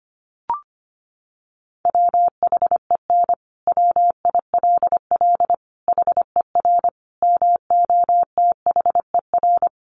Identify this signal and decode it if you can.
Morse code, heard as a keyed tone, about 25 wpm, 700 Hz. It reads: WHEN WILL HER MOTHER
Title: Sentences from Top 200 Words 25wpm